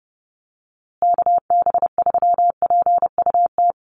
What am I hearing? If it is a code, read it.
Morse code, heard as a keyed tone, about 30 wpm, 700 Hz. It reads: KB3PUT
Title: USA Call Signs Repeated 3x 30wpm